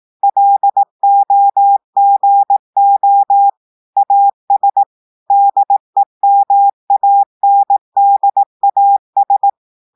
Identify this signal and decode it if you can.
Morse code, heard as a keyed tone, about 18 wpm, 800 Hz. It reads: LOGO AS DEMANDAS
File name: audios / portugais / portugais texte en clair 2 vitesse 18 mots minutes